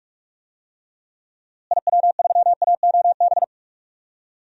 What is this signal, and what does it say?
Morse code, heard as a keyed tone, about 46 wpm, 700 Hz. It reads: IW3AOB
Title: DXCC-100 Call Signs - 6 Characters - Repeated 2x - 45wpm